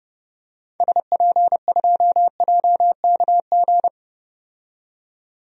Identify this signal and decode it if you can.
Morse code, heard as a keyed tone, about 31 wpm, 700 Hz. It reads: SP2JKG
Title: DXCC-100 Call Signs - 6 Characters - 30wpm